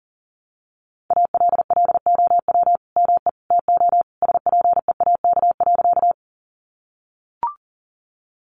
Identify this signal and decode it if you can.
Morse code, heard as a keyed tone, about 40 wpm, 700 Hz. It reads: ALLOW ME TO SPEAK.
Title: Sentences from Top 1100 Words 40wpm